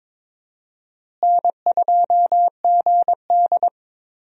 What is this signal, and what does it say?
Morse code, heard as a keyed tone, about 22 wpm, 700 Hz. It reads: N2GD
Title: USA Call Signs - 4 Characters - Repeated 2x - 22wpm